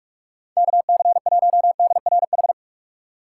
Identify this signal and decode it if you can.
Morse code, heard as a keyed tone, about 44 wpm, 700 Hz. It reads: KC1DRH
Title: USA Call Signs - 6 Characters - 4x Character Spacing - 45wpm